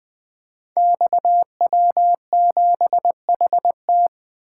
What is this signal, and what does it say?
Morse code, heard as a keyed tone, about 20 wpm, 700 Hz. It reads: XW7HT
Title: Code Groups - 5 Letter:Numbers - 20wpm